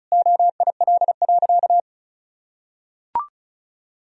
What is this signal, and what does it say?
Morse code, heard as a keyed tone, about 36 wpm, 700 Hz. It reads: OIL.
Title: Sentences from Top 1000 Words - Review Part 4 - 35wpm